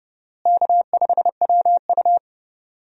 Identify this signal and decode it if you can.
Morse code, heard as a keyed tone, about 30 wpm, 700 Hz. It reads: K5WU
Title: USA Call Signs - 4 Characters - Repeated 3x - 30wpm